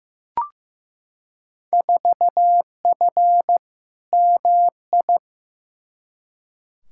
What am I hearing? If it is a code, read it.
Morse code, heard as a keyed tone, about 15 wpm, 700 Hz. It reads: 4F MI
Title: Field Day Exchanges - 15wpm